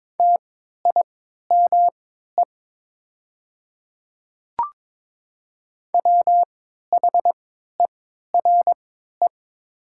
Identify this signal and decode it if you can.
Morse code, heard as a keyed tone, about 22 wpm, spaced as for 12 wpm, 700 Hz. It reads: TIME WHERE
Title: 3x Character Spacing - Top 200 Words - 22wpm